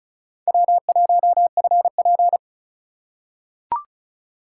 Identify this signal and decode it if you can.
Morse code, heard as a keyed tone, about 34 wpm, 700 Hz. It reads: W1FP
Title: USA Call Signs - 4 Characters - 4x Character Spacing - 35wpm